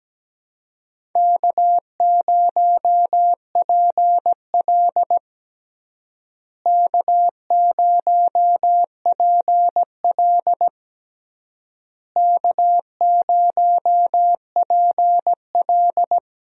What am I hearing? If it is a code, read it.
Morse code, heard as a keyed tone, about 17 wpm, 700 Hz. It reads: K0PL K0PL K0PL
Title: USA Call Signs - 4 Characters - Repeated 3x - 17wpm